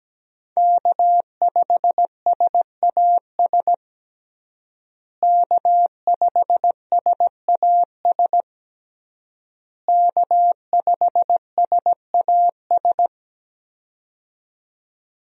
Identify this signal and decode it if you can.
Morse code, heard as a keyed tone, about 17 wpm, 700 Hz. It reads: K5SAS K5SAS K5SAS
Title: USA Call Signs - 5 Characters - Repeated 3x - 17wpm